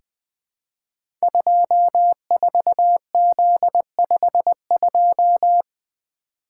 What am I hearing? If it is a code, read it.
Morse code, heard as a keyed tone, about 21 wpm, 700 Hz. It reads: 24Z52
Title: Code Groups - 5 Letter:Numbers - 20wpm